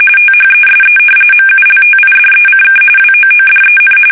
RTTY, about 45.45 baud. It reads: THESE MULTIDECK WOODEN
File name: fsk850 / battleshipnj